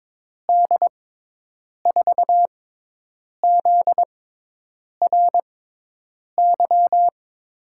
Morse code, heard as a keyed tone, about 22 wpm, spaced as for 8 wpm, 700 Hz. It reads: D4ZRY
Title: USA Call Signs - 6x Character Spacing - 22wpm